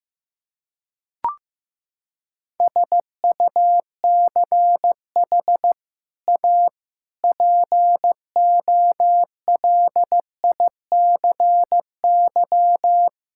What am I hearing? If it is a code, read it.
Morse code, heard as a keyed tone, about 16 wpm, 700 Hz. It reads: SUCH A POLICY
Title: Sentences from Top 1000 Words - Review Part 5 - 15wpm